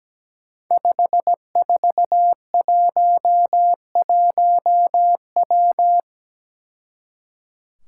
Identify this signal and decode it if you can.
Morse code, heard as a keyed tone, about 17 wpm, 700 Hz. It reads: 5411W
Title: Five Challenging Letter-Number Combinations 17wpm